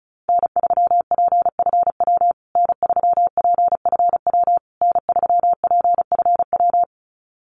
Morse code, heard as a keyed tone, about 34 wpm, 700 Hz. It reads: N3PFW N3PFW N3PFW
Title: USA Call Signs Repeated 3x 35wpm